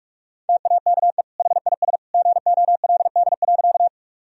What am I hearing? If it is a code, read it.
Morse code, heard as a keyed tone, about 42 wpm, 700 Hz. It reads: TAKE HIS GOLD.
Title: Sentences from Top 1000 Words - Review Part 3 - 45wpm